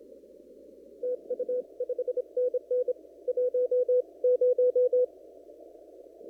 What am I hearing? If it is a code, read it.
Morse code, heard as a keyed tone, about 25 wpm, 500 Hz. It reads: TU 5NN 10
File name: CW / kt5j wpx cw-op-k5pi